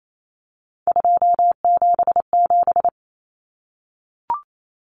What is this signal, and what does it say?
Morse code, heard as a keyed tone, about 28 wpm, 700 Hz. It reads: 277
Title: Common Cut Numbers - 3-Digit 28wpm